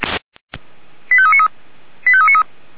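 Five-tone selective call (ZVEI), heard as quick runs of five tones, tones of about 70 ms. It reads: 85393 85393
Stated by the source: ZVEI five-tone selective call 89155 received on an SDR.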